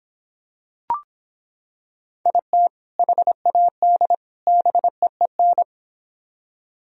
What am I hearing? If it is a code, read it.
Morse code, heard as a keyed tone, about 26 wpm, 700 Hz. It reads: IT HAD BEEN